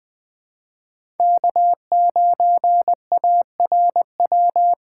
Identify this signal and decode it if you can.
Morse code, heard as a keyed tone, about 20 wpm, 700 Hz. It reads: K9ARW
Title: USA Call Signs - 4x Character Spacing - 20wpm